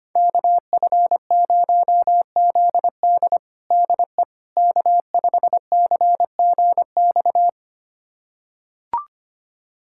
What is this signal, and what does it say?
Morse code, heard as a keyed tone, about 25 wpm, 700 Hz. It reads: KF0ZD DE K5CGX
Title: USA Callsign DE Callsign - 5 Characters - 25wpm